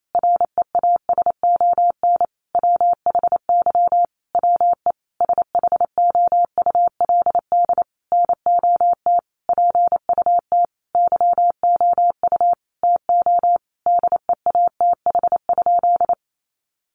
Morse code, heard as a keyed tone, about 28 wpm, 700 Hz. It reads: REASON WHY WE SHOULD NOT PUT YOU TO DEATH?
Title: Sentences from Top 1100 Words 28wpm